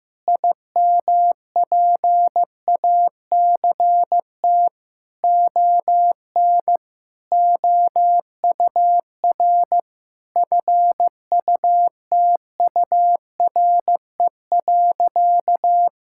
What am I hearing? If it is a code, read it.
Morse code, heard as a keyed tone, about 15 wpm, 700 Hz. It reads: IMPACT ON OUR FUTURE.
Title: Sentences from Top 1000 Words - Review Part 6 - 15wpm